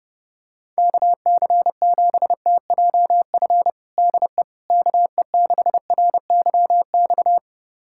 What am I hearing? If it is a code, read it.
Morse code, heard as a keyed tone, about 30 wpm, 700 Hz. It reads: KC7TJF DE KE6RYX
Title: USA Callsign DE Callsign - 6 Characters - 30wpm